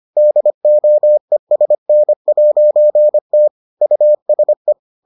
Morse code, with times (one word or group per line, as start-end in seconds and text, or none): 0.2-3.5 DOESN'T
3.8-4.7 USE